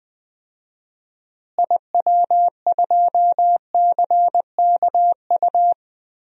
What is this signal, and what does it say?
Morse code, heard as a keyed tone, about 20 wpm, 700 Hz. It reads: IW2CKU
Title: DXCC-100 Call Signs - 6 Characters - Repeated 2x - 20wpm